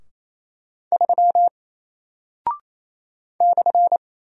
Morse code, heard as a keyed tone, about 28 wpm, 700 Hz. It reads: 3 /
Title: Single Character (Letter:Number:Punctuation:Prosign) Alt 28wpm